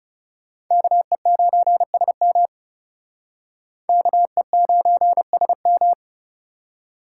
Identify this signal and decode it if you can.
Morse code, heard as a keyed tone, about 39 wpm, 700 Hz. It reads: KE9SM KE9SM